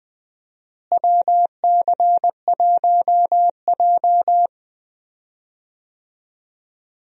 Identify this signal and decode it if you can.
Morse code, heard as a keyed tone, about 20 wpm, 690 Hz. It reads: WC1J
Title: USA Call Signs - 4 Characters - Repeated 3x - 20wpm